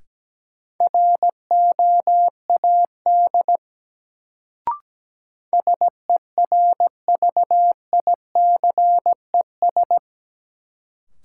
Morse code, heard as a keyed tone, about 17 wpm, 700 Hz. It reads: ROAD SERVICES
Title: Sets of 5 Words - New Word Drill - 17wpm